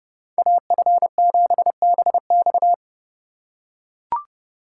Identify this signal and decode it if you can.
Morse code, heard as a keyed tone, about 30 wpm, 700 Hz. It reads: AF7BX